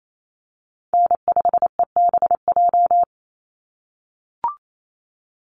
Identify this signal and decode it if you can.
Morse code, heard as a keyed tone, about 28 wpm, 700 Hz. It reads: N5EBJ
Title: USA Call Signs 28wpm